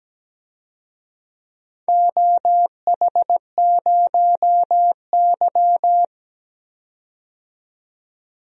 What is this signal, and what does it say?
Morse code, heard as a keyed tone, about 17 wpm, 700 Hz. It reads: OH0Y
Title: DXCC-100 Call Signs - 4 Characters - 17wpm